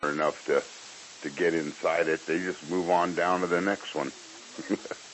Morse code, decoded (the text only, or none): none